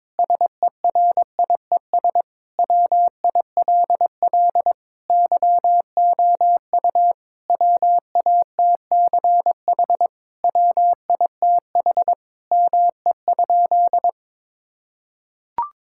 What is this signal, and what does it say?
Morse code, heard as a keyed tone, about 22 wpm, 700 Hz. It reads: SERIES WILL YOU WATCH WITH ME?
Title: Sentences from Top 1000 Words - Review Part 5 - 22wpm